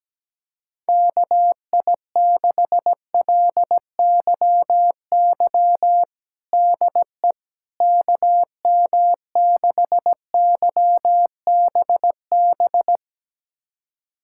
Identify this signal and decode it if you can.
Morse code, heard as a keyed tone, about 17 wpm, 700 Hz. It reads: KI6LYY DE KM6YBB
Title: USA Callsign DE Callsign - 6 Characters - 17wpm